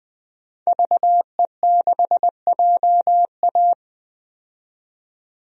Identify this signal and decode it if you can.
Morse code, heard as a keyed tone, about 20 wpm, 700 Hz. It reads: VE6JA
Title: DXCC-100 Call Signs - 5 Characters - 20wpm